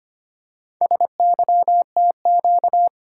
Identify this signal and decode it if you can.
Morse code, heard as a keyed tone, about 25 wpm, 690 Hz. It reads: SYTQ
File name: practice / 25wpm / Four Letter-Number Combinations - 25wpm